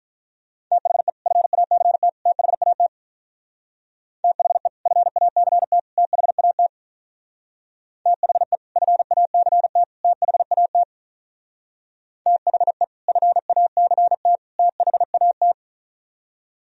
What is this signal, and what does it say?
Morse code, heard as a keyed tone, about 43 wpm, 700 Hz. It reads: THE FACT THAT THE FACT THAT THE FACT THAT THE FACT THAT
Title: Speed-Racing - Sets of 3 Words 35wpm